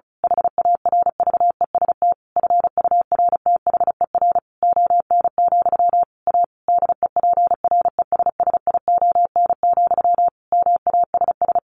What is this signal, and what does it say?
Morse code, heard as a keyed tone, about 34 wpm, 700 Hz. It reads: HARVEST FURTHER ON, A DEPRESSION, MASS
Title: A Sound Of Thunder 35wpm